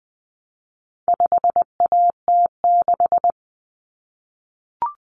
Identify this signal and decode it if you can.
Morse code, heard as a keyed tone, about 20 wpm, 700 Hz. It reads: 5AT6
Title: Common Cut Numbers - 4-Digit 20wpm